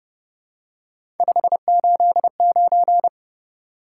Morse code, heard as a keyed tone, about 30 wpm, 700 Hz. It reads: 589
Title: Top 180 QSO Elements 30wpm